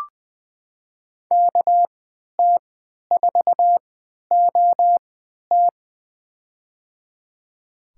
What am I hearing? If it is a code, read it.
Morse code, heard as a keyed tone, about 20 wpm, spaced as for 11 wpm, 700 Hz. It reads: KT4OT
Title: USA Call Signs - 5 Characters - 3x Character Spacing - 20wpm